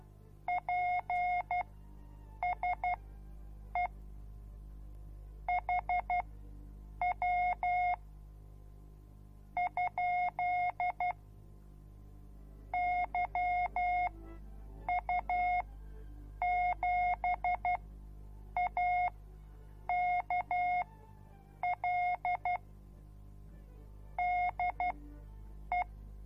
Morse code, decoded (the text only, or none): PSE HW ? YU7AKL DE